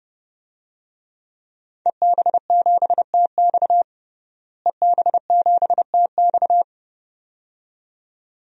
Morse code, heard as a keyed tone, about 30 wpm, 700 Hz. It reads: EB7TX EB7TX
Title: DXCC-100 Call Signs - 5 Characters - Repeated 2x - 30wpm